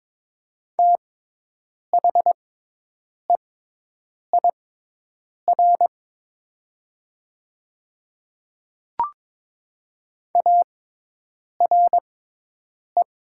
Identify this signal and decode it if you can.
Morse code, heard as a keyed tone, about 22 wpm, spaced as for 8 wpm, 700 Hz. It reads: THEIR ARE
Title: 6x Character Spacing - Top 100 Words - 22wpm